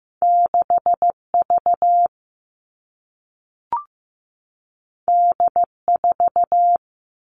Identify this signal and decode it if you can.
Morse code, heard as a keyed tone, about 15 wpm, 700 Hz. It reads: 6V D4
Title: All Cut Numbers - 2-Digit 15wpm